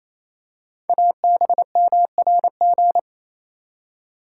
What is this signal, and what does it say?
Morse code, heard as a keyed tone, about 27 wpm, 700 Hz. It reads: ABMRG
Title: Code Groups - 5 Letters - 28wpm